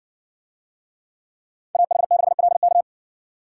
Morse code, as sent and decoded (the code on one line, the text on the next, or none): .- .... -.... .-.. -.-
AH6LK